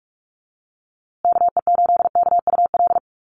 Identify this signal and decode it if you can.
Morse code, heard as a keyed tone, about 44 wpm, 700 Hz. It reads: KE8KUL
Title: USA Call Signs Repeated 3x 45wpm